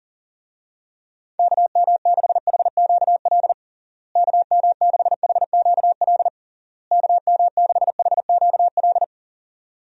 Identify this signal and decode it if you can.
Morse code, heard as a keyed tone, about 37 wpm, 700 Hz. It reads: KM6HQL KM6HQL KM6HQL
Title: USA Call Signs - 6 Characters - Repeated 3x - 40wpm